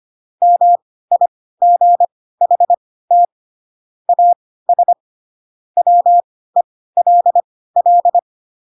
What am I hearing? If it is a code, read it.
Morse code, heard as a keyed tone, about 25 wpm, spaced as for 16 wpm, 700 Hz. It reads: MIGHT AS WELL
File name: Pinocchio / Pinocchio ch3 10 25 ik1hge